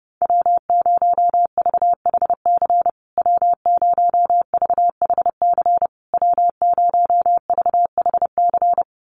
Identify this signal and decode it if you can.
Morse code, heard as a keyed tone, about 30 wpm, 700 Hz. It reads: W0VHC W0VHC W0VHC
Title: USA Call Signs Repeated 3x 30wpm